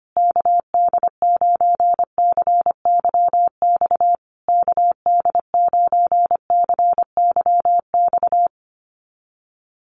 Morse code, heard as a keyed tone, about 25 wpm, 700 Hz. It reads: KD9CYX KD9CYX
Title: USA Call Signs Repeated 3x 25wpm